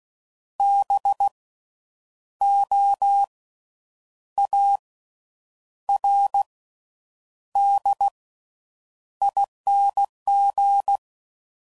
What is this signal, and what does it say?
Morse code, heard as a keyed tone, about 16 wpm, 800 Hz. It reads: B O A R D ING
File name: Suffixes / Special INGs / Slow With Faster Ings